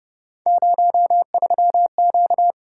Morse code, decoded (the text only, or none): 03Q